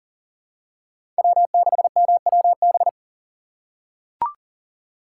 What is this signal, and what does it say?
Morse code, heard as a keyed tone, about 40 wpm, 700 Hz. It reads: W6MWB
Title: USA Call Signs - 5 Characters - 4x Character Spacing - 40wpm